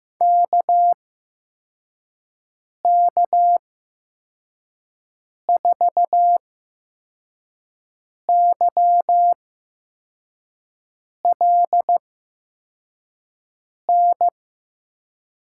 Morse code, heard as a keyed tone, about 15 wpm, spaced as for 4 wpm, 700 Hz. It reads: KK4YLN
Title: USA Call Signs - 8x Character Spacing - 15wpm